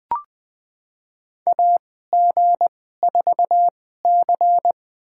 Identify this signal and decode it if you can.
Morse code, heard as a keyed tone, about 20 wpm, spaced as for 15 wpm, 700 Hz. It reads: AG4C
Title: USA Call Signs - 4 Characters - 2x Character Spacing - 20wpm